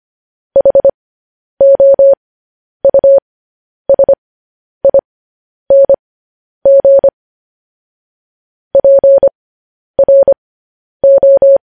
Morse code, heard as a keyed tone, about 25 wpm, spaced as for 10 wpm, 550 Hz. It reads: HOUSING PRO